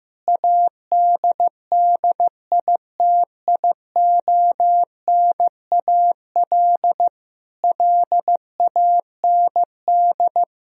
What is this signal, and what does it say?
Morse code, heard as a keyed tone, about 15 wpm, 700 Hz. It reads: ADDITIONAL LAND
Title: Sentences from Top 1000 Words - Review Part 4 - 15wpm